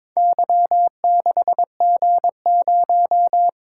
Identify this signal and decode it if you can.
Morse code, heard as a keyed tone, about 22 wpm, 700 Hz. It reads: Y6G0
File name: Four Letter-Number Combinations - 22wpm